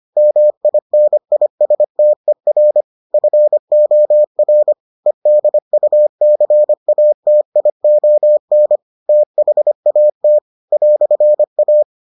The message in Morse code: MINISTER FOR EDUCATION THAT "A

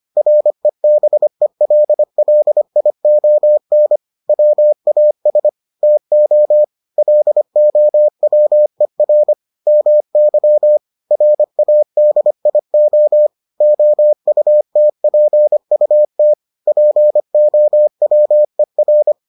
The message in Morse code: REBELLION WAS TO LOWER MY RADIO OUTPUT POWER

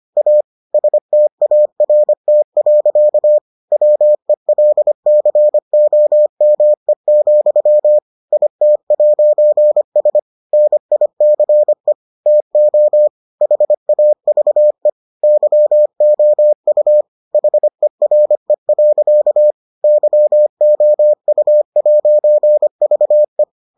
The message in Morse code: A START. WELCOME, IT'S NICE TO HAVE YOU HERE. YOU'VE

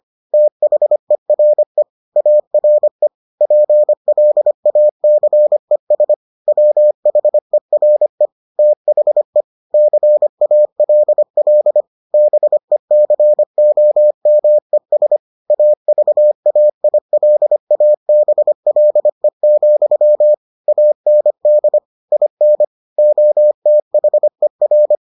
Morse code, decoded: THERE ARE PLACES WHERE THE CALL BECOMES AVAILABLE, AND IN OTHER

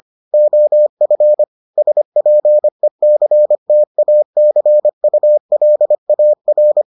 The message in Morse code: OF SPECTACULAR

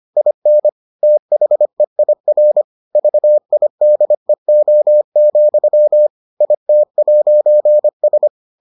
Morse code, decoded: IN THEIR VIDEO, IT'S

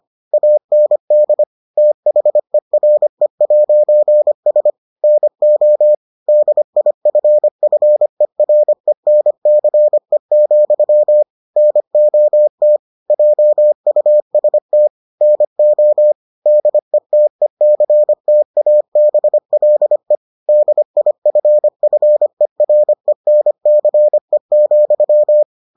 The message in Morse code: AND THERE'S NO DIFFERENCE, NOT JUST NO DETECTABLE DIFFERENCE,

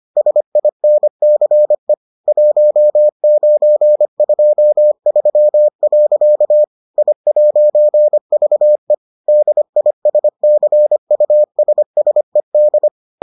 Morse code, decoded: SINCE 1923. I'VE DISCUSSED

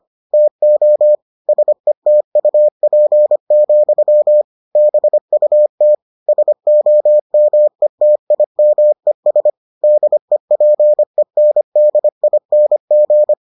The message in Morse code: TO SETUP, BUT SOMETIMES DEPENDING

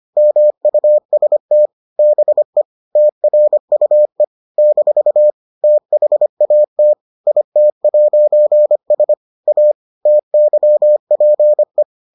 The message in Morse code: MUST BE TRUE - THAT IT'S A TYPE

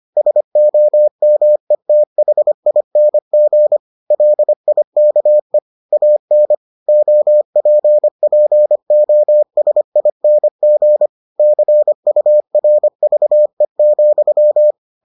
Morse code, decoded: SOMETHING LIKE AN OPPOSING CURVE,

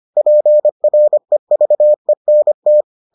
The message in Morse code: PREVENT

